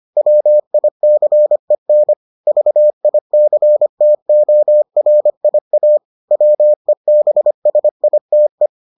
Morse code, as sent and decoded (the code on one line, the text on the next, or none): .-- .. -.-. -. / ...- .. -.-. - --- .-. .. .- / .-- . -... ... .. - .
WICEN VICTORIA WEBSITE